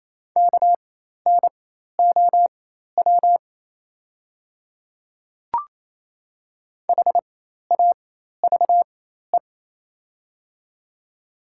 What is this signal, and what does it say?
Morse code, heard as a keyed tone, about 28 wpm, spaced as for 13 wpm, 690 Hz. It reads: KNOW HAVE
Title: 4x Character Spacing - Top 200 Words - 28wpm